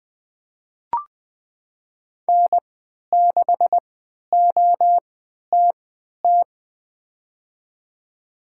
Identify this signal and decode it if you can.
Morse code, heard as a keyed tone, about 20 wpm, spaced as for 11 wpm, 700 Hz. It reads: N6OTT
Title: USA Call Signs - 5 Characters - 3x Character Spacing - 20wpm